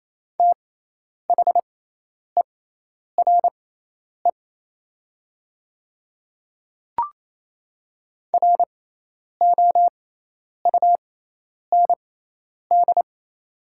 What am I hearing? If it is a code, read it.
Morse code, heard as a keyed tone, about 28 wpm, spaced as for 10 wpm, 700 Hz. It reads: THERE ROUND
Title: 6x Character Spacing - Top 200 Words - 28wpm